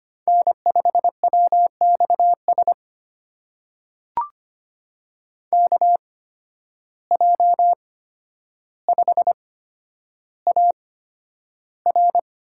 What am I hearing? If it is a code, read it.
Morse code, heard as a keyed tone, about 25 wpm, spaced as for 11 wpm, 700 Hz. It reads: N5WXS KJ5AR
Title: USA Call Signs - 5 Characters - 8x Character Spacing - 25wpm